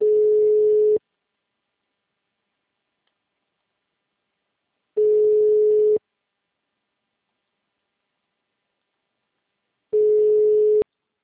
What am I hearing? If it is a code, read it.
Ringback tone (Europe, 425 Hz).